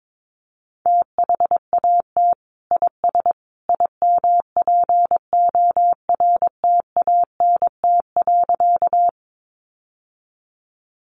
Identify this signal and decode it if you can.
Morse code, heard as a keyed tone, about 22 wpm, 700 Hz. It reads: THAT IS IMPORTANT.
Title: Sentences from Top 600 Words 22wpm